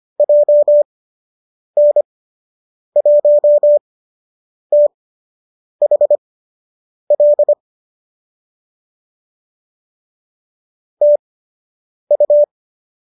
Morse code, heard as a keyed tone, about 25 wpm, spaced as for 8 wpm, 590 Hz. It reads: JN1THL TU
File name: qso003 08